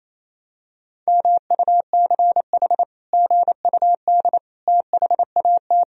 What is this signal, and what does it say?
Morse code, heard as a keyed tone, about 28 wpm, 700 Hz. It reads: MUCH GUD THAT
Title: Sentences from Top 200 Words and Abbreviations 28wpm